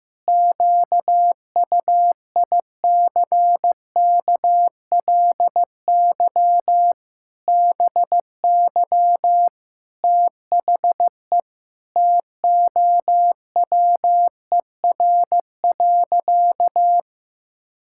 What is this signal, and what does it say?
Morse code, heard as a keyed tone, about 15 wpm, 700 Hz. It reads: QUICKLY BY THE TOWER.
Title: Pangram Challenge v2 - 15wpm